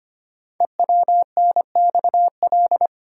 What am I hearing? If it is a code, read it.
Morse code, heard as a keyed tone, about 25 wpm, 690 Hz. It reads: EWNXL
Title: Code Groups - 5 Letters - 25wpm